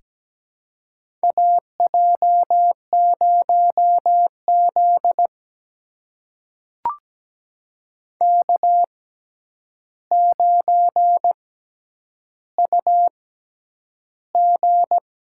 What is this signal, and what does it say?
Morse code, heard as a keyed tone, about 17 wpm, spaced as for 9 wpm, 700 Hz. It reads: AJ0Z K9UG
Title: USA Call Signs - 4 Characters - 6x Character Spacing - 17wpm